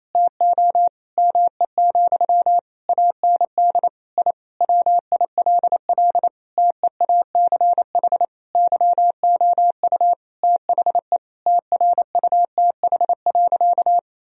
Morse code, heard as a keyed tone, about 28 wpm, 700 Hz. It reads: TO ME, AND I WILL TEACH YOU THE TRUTH.
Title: Sentences from Top 1000 Words - Review Part 2 - 28wpm